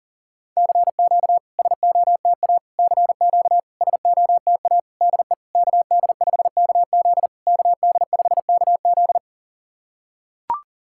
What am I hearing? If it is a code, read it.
Morse code, heard as a keyed tone, about 40 wpm, 690 Hz. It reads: CQ SOTA CQ SOTA DE KD5KZ KD5KZ
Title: SOTA QSO Practice 40wpm